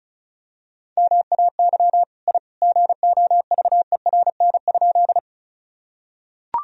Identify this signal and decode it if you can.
Morse code, heard as a keyed tone, about 35 wpm, 700 Hz. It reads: MAY I GOVERN?
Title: Sentences from Top 1000 Words - Review Part 2 - 35wpm